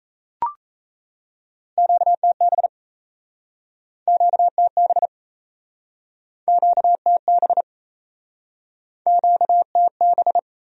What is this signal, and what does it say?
Morse code, heard as a keyed tone, about 37 wpm, 700 Hz. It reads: QTB QTB QTB QTB